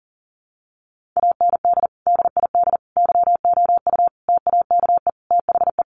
Morse code, heard as a keyed tone, about 40 wpm, 700 Hz. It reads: AND DID YOU TAKE THE
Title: Sentences from Top 1500 Words 40wpm